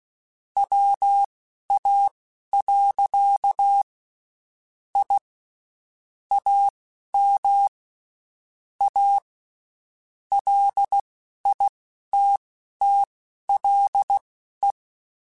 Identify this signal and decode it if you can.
Morse code, heard as a keyed tone, about 16 wpm, spaced as for 11 wpm, 800 Hz. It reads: WA. I AM A LITTLE
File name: (spaced) / QSOs Spaced-mp3s / kc7haa @15 sp